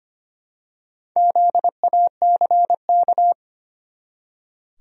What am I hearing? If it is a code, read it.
Morse code, heard as a keyed tone, about 25 wpm, 700 Hz. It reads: ZACK
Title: USA Common Names - Max 5 Letters - 25wpm